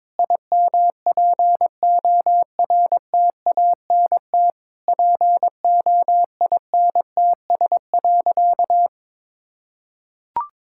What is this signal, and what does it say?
Morse code, heard as a keyed tone, about 21 wpm, 690 Hz. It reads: IMPORTANT POINTS.
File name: practice / 22wpm / Easy Sentences with Hints - 22wpm